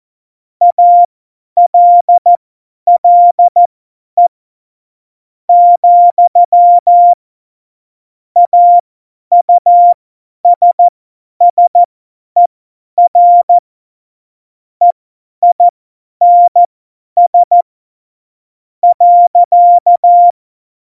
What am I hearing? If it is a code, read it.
Morse code, heard as a keyed tone, about 14 wpm, spaced as for 10 wpm, 700 Hz. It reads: ALLE , AUSSER EINS .
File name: Morsekurs / Witze / Witz 23. Oktober 2018, 50 F70